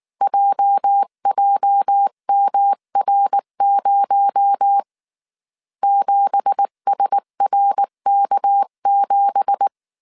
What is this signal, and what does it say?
Morse code, heard as a keyed tone, about 19 wpm, 800 Hz. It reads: JJMR0 7SRK7
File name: CW / DET / 1 9 1500C S C 18wpm 2539